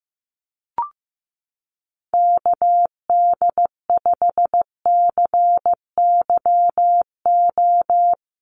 Morse code, heard as a keyed tone, about 15 wpm, 700 Hz. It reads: KD5CYO
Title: USA Call Signs Repeated 3x 15wpm